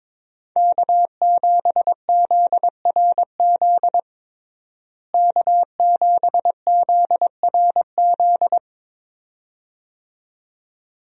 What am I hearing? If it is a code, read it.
Morse code, heard as a keyed tone, about 22 wpm, 700 Hz. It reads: K7ZRZ K7ZRZ
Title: USA Call Signs - 5 Characters - Repeated 2x - 22wpm